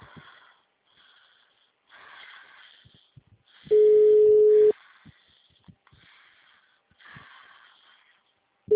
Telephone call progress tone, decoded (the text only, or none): ringback tone